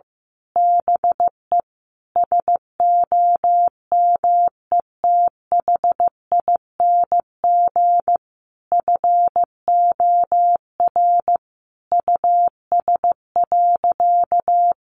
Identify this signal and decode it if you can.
Morse code, heard as a keyed tone, about 15 wpm, 700 Hz. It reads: BE SOMETHING FOR US.
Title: Sentences from Top 1200 Words 15wpm